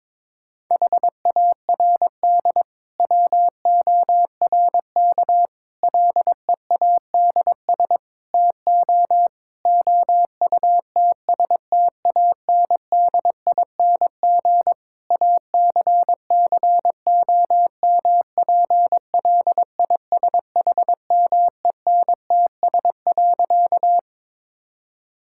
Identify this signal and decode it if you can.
Morse code, heard as a keyed tone, about 22 wpm, 700 Hz. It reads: HARD WORK LEADS TO OUTSTANDING ACCOMPLISHMENTS.
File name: Difficult Sentences with Hints - 22wpm